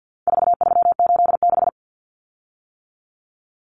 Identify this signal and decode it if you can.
Morse code, heard as a keyed tone, about 50 wpm, 700 Hz. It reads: 4386
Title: Common Cut Numbers - 4-Digit 50wpm